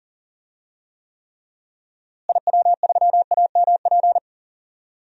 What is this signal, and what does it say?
Morse code, heard as a keyed tone, about 40 wpm, 700 Hz. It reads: IW3AMP
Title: DXCC-100 Call Signs - 6 Characters - Repeated 3x - 40wpm